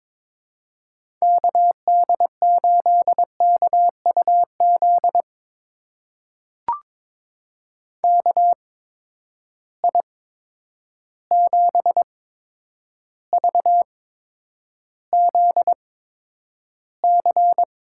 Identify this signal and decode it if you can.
Morse code, heard as a keyed tone, about 22 wpm, spaced as for 9 wpm, 700 Hz. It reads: KD8KUZ KI7VZC
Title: USA Call Signs - 6 Characters - 8x Character Spacing - 22wpm